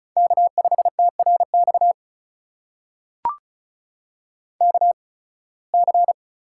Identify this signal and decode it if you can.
Morse code, heard as a keyed tone, about 36 wpm, spaced as for 23 wpm, 700 Hz. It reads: K5TRX KC
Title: USA Call Signs - 8x Character Spacing - 35wpm